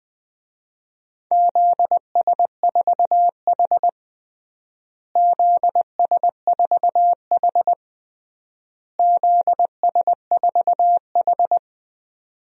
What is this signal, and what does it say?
Morse code, heard as a keyed tone, about 21 wpm, 700 Hz. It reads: ZS4H ZS4H ZS4H
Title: DXCC-100 Call Signs - 4 Characters - Repeated 3x - 20wpm